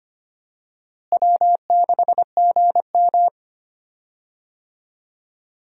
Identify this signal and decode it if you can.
Morse code, heard as a keyed tone, about 25 wpm, 700 Hz. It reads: W6GM
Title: USA Call Signs - 4 Characters - Repeated 2x - 25wpm